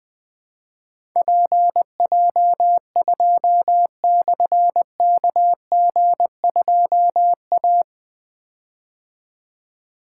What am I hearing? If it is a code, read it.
Morse code, heard as a keyed tone, about 20 wpm, 700 Hz. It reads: PJ2/KG2A
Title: Difficult Call Signs 20wpm